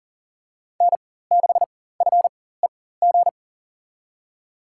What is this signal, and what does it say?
Morse code, heard as a keyed tone, about 40 wpm, spaced as for 19 wpm, 700 Hz. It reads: N6FEG
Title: USA Call Signs - 5 Characters - 4x Character Spacing - 40wpm